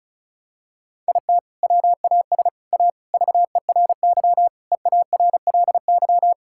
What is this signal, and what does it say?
Morse code, heard as a keyed tone, about 36 wpm, 700 Hz. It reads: IT WAS A VERY EARLY
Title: Sentences from Top 1000 Words - Review Part 6 - 35wpm